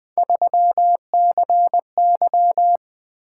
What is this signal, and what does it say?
Morse code, heard as a keyed tone, about 20 wpm, 700 Hz. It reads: 3CY
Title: Three Letter-Number Combinations - 20wpm